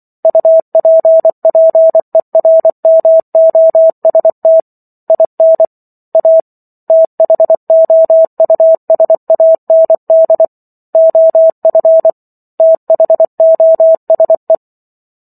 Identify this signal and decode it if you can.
Morse code, heard as a keyed tone, about 24 wpm, 650 Hz. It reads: UPPERMOST IN A THOUSAND OF THOSE